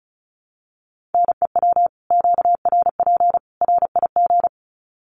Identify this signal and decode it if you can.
Morse code, heard as a keyed tone, about 34 wpm, 700 Hz. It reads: NEW QRP RIG
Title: Common QSO Phrases 35wpm